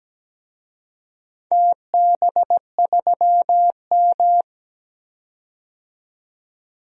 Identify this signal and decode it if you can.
Morse code, heard as a keyed tone, about 17 wpm, 700 Hz. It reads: TB3M
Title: DXCC-100 Call Signs - 4 Characters - 17wpm